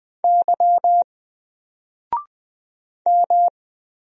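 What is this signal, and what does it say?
Morse code, heard as a keyed tone, about 20 wpm, 700 Hz. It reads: Y M
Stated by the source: Single letter and number Morse drill at 20 wpm, from N4LCD.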